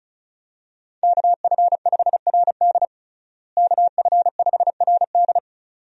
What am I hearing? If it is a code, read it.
Morse code, heard as a keyed tone, about 34 wpm, 700 Hz. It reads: KF5RD KF5RD
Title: USA Call Signs - 5 Characters - Repeated 2x - 35wpm